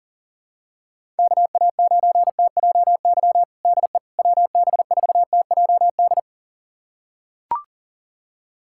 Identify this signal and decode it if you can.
Morse code, heard as a keyed tone, about 40 wpm, 700 Hz. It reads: KA9TJY DE WB4TJD
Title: USA Callsign DE Callsign 40wpm